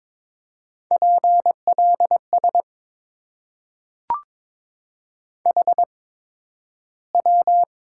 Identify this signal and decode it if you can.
Morse code, heard as a keyed tone, about 22 wpm, spaced as for 12 wpm, 690 Hz. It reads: PLS HW
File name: Top 90 QSO Elements - 8x Character Spacing - 22wpm